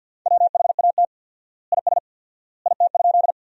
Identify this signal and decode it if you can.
Morse code, heard as a keyed tone, about 50 wpm, 700 Hz. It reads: WHAT IS IT?
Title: 4x Word-Spacing - Sentences from Top 100 Words - 50wpm